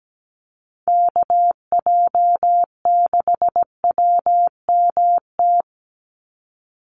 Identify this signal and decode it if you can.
Morse code, heard as a keyed tone, about 17 wpm, 700 Hz. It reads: KJ6WMT
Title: USA Call Signs 17wpm